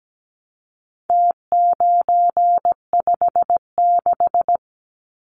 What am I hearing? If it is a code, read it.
Morse code, heard as a keyed tone, about 17 wpm, 700 Hz. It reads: T956
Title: Numbers - Contest Style - 4-Digit 17wpm